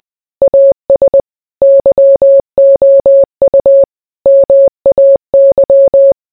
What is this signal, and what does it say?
Morse code, heard as a keyed tone, about 20 wpm, 550 Hz. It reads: AS YOU MAY